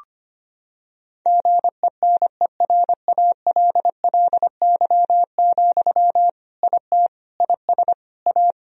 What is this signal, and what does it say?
Morse code, heard as a keyed tone, about 25 wpm, 700 Hz. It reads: GENERALLY, IT IS A
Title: Sentences from Top 1000 Words - Review Part 4 - 25wpm